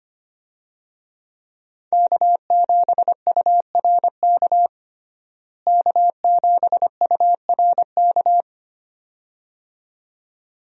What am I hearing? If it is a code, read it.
Morse code, heard as a keyed tone, about 25 wpm, 700 Hz. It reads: K7URK K7URK
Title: USA Call Signs - 5 Characters - Repeated 2x - 25wpm